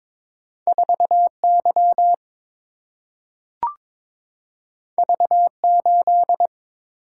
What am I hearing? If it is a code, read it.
Morse code, heard as a keyed tone, about 22 wpm, 700 Hz. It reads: 4 Y V 8
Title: Single-Letter Single-Number 22wpm